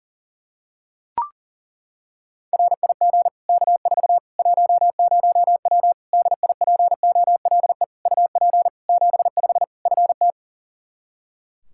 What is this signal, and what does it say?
Morse code, heard as a keyed tone, about 40 wpm, 700 Hz. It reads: RIG K4 10W DIPOLE UP 75 FT
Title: Practice QSO - Exchange 2 - Info - 40wpm